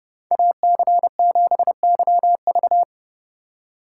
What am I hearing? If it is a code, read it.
Morse code, heard as a keyed tone, about 35 wpm, 700 Hz. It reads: AC7YV
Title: Speed Racing - USA 2x2 Call Signs 30wpm